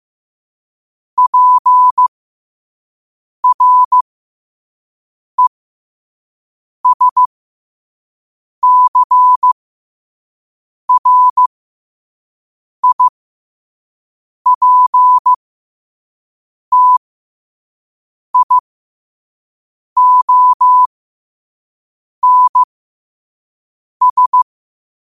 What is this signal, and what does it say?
Morse code, heard as a keyed tone, about 15 wpm, spaced as for 5 wpm, 1000 Hz. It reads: PRESCRIPTIONS
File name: qotdfiles / 5wpm